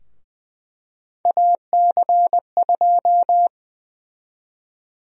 Morse code, heard as a keyed tone, about 20 wpm, 700 Hz. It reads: AC2
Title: USA Call Sign Prefixes - 20wpm